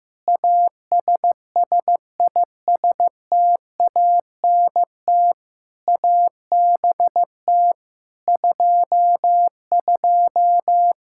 Morse code, heard as a keyed tone, about 15 wpm, 700 Hz. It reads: ASSISTANT ABT 22
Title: Practice QSO - Exchange 3 - More Info - 15wpm